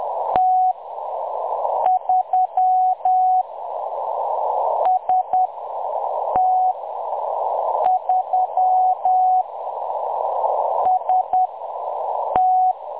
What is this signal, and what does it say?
Morse code, heard as a keyed tone, about 10 wpm, spaced as for 6 wpm, 740 Hz. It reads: T3 ST3 ST
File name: M23-ST3- -0858utc-10184khz